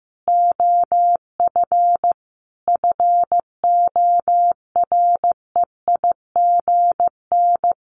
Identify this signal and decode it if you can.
Morse code, heard as a keyed tone, about 15 wpm, 700 Hz. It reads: OF FOREIGN
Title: Sentences from Top 1400 Words 15wpm